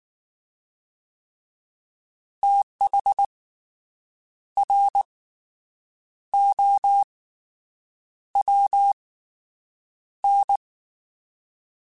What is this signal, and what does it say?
Morse code, heard as a keyed tone, about 19 wpm, 800 Hz. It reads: TH R O W N
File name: Fast TH @18